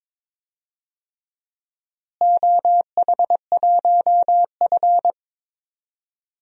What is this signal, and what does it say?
Morse code, heard as a keyed tone, about 22 wpm, 700 Hz. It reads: OH1F
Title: DXCC-100 Call Signs - 4 Characters - Repeated 3x - 22wpm